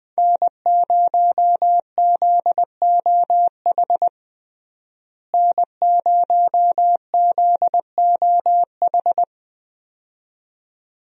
Morse code, heard as a keyed tone, about 20 wpm, 700 Hz. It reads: N0ZOH N0ZOH
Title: USA Call Signs - 5 Characters - Repeated 2x - 20wpm